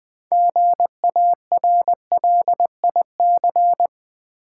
Morse code, heard as a keyed tone, about 20 wpm, 690 Hz. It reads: GARLIC
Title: Difficult Words with Hints - 20wpm